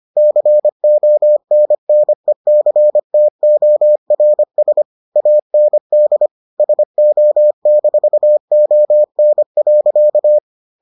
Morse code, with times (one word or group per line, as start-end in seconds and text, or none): 0.2-4.8 CONNECTORS
5.2-6.3 AND
6.6-10.4 SO-ON.